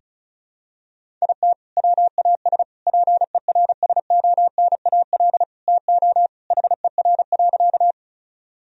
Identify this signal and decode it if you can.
Morse code, heard as a keyed tone, about 36 wpm, 700 Hz. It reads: IT WAS PERSONAL TO HER.
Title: Sentences from Top 1000 Words - Review Part 4 - 35wpm